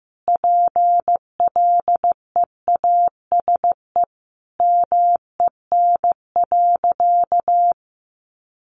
Morse code, heard as a keyed tone, about 15 wpm, 700 Hz. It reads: PLEASE MEN.
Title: Sentences from Top 1500 Words 15wpm